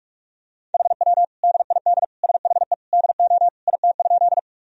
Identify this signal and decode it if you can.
Morse code, heard as a keyed tone, about 46 wpm, 700 Hz. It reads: HW DID SHE DO IT?
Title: Sentences from Top 100 Words and Abbreviations 45wpm